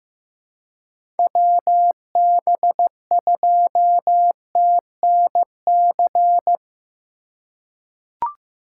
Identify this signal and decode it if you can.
Morse code, heard as a keyed tone, about 15 wpm, 700 Hz. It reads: WB2TNC
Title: USA Call Signs - 4x Character Spacing - 15wpm